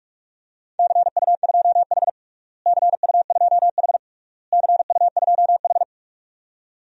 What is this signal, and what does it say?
Morse code, heard as a keyed tone, about 46 wpm, 700 Hz. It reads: CU2H CU2H CU2H
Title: DXCC-100 Call Signs - 4 Characters - Repeated 3x - 45wpm